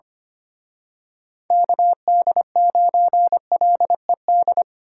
Morse code, heard as a keyed tone, about 29 wpm, 700 Hz. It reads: KD9LED
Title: Speed Racing - USA Call Signs 25wpm